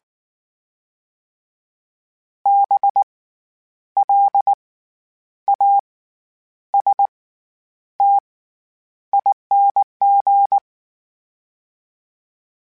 Morse code, heard as a keyed tone, about 19 wpm, spaced as for 9 wpm, 800 Hz. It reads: BLASTING
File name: Special INGs / Special INGs-mp3s / SlowToFaster ing @18